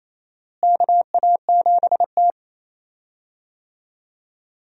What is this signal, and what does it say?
Morse code, heard as a keyed tone, about 28 wpm, 700 Hz. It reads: KA7T
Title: USA Call Signs - 4 Characters - Repeated 2x - 28wpm